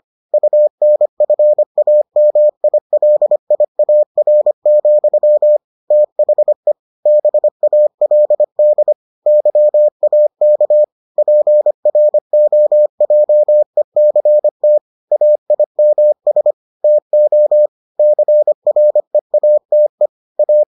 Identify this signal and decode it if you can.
Morse code, heard as a keyed tone, about 25 wpm, 600 Hz. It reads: UNFAMILIAR, THE BALD YAK PROJECT AIMS TO CREATE A